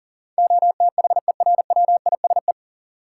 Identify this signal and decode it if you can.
Morse code, heard as a keyed tone, about 40 wpm, 700 Hz. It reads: OTHERWISE
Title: Sentences from Top 1200 Words - New Word Drill Alt - 40wpm